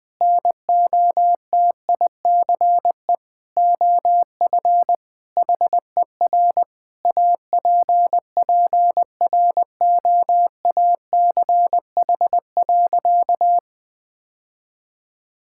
Morse code, heard as a keyed tone, about 20 wpm, 700 Hz. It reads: NOTICE OF HER APPROACH.
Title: Sentences from Top 1000 Words - Review Part 6 - 20wpm